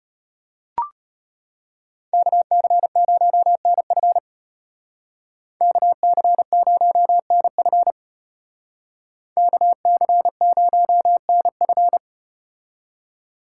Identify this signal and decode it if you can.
Morse code, heard as a keyed tone, about 34 wpm, 700 Hz. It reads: KC0NF KC0NF KC0NF